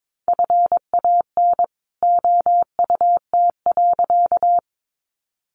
Morse code, heard as a keyed tone, about 22 wpm, 700 Hz. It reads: FAN OUT.
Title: Sentences from Top 1300 Words 22wpm